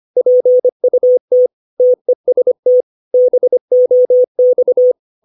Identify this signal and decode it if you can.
Morse code, heard as a keyed tone, about 25 wpm, 500 Hz. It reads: PUT TEST BOX
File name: 632 words / 632 Words 25WPM